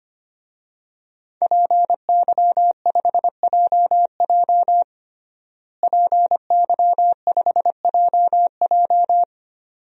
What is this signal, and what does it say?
Morse code, heard as a keyed tone, about 25 wpm, 700 Hz. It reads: PY5JJ PY5JJ
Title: DXCC-100 Call Signs - 5 Characters - Repeated 2x - 25wpm